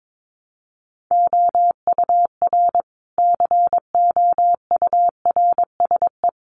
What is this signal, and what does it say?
Morse code, heard as a keyed tone, about 22 wpm, 700 Hz. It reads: OUR COURSE
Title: Sentences from Top 1300 Words 22wpm